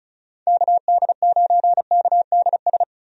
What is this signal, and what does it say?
Morse code, heard as a keyed tone, about 36 wpm, 700 Hz. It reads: KD9KDS
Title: USA Call Signs - 6 Characters - 35wpm